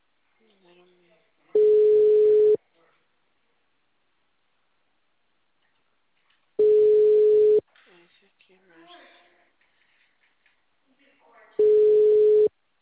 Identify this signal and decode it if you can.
Ringback tone (Europe, 425 Hz), with speech.